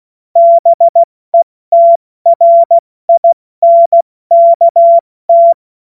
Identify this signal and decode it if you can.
Morse code, heard as a keyed tone, about 16 wpm, 680 Hz. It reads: BETRINKT
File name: Morsekurs / Witze / Witz 08. Februar 2022 70 F80